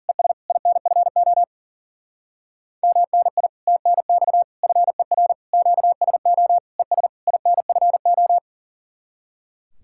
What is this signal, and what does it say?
Morse code, heard as a keyed tone, about 50 wpm, 700 Hz. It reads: ES INFO MNI TNX FER QSO ES INFO
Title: Speed-Racing Common QSO Phrases Encore - 40wpm